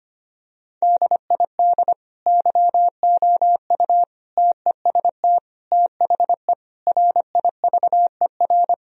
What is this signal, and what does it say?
Morse code, heard as a keyed tone, about 25 wpm, 700 Hz. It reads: DID YOU TEST THE RIVER
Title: Sentences from Top 1000 Words - Review Part 4 - 25wpm